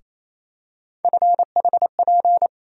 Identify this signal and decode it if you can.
Morse code, heard as a keyed tone, about 28 wpm, 700 Hz. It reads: FHP
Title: USA Call Sign Postfixes - 3 Letters 28wpm